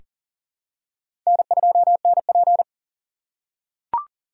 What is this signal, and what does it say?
Morse code, heard as a keyed tone, about 40 wpm, 700 Hz. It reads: N2NP
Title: USA Call Signs - 4x Character Spacing - 40wpm